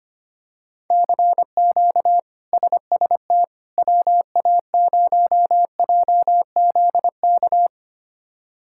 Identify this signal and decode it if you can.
Morse code, heard as a keyed tone, about 25 wpm, 700 Hz. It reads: CQ SST WA0JZK
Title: K1USN SST QSO Practice - 25wpm